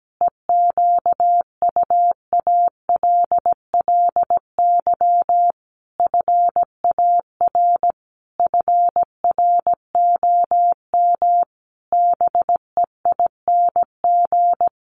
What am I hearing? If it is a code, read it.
Morse code, heard as a keyed tone, about 17 wpm, 700 Hz. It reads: EQUALLY FAR FROM BEING
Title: Sentences from Top 1400 Words 17wpm